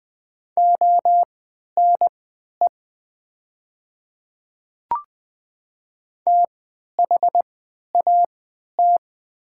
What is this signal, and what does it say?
Morse code, heard as a keyed tone, about 20 wpm, spaced as for 11 wpm, 700 Hz. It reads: ONE THAT
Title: 3x Character Spacing - Top 100 Words - 20wpm